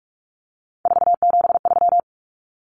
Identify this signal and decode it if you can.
Morse code, heard as a keyed tone, about 44 wpm, 700 Hz. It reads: 473
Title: Numbers - 3-Digit 45wpm